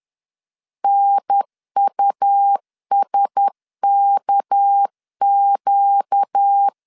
Morse code, heard as a keyed tone, about 10 wpm, 800 Hz. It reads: NUSKQ
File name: LEZIONI / 6 PKX0 / 6 PKX 10wpm 2426